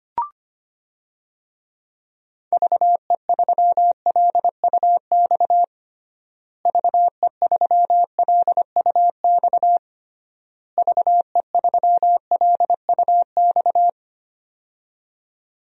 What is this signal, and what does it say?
Morse code, heard as a keyed tone, about 25 wpm, 700 Hz. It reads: VE3LUX VE3LUX VE3LUX
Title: DXCC-100 Call Signs - 6 Characters - Repeated 3x - 25wpm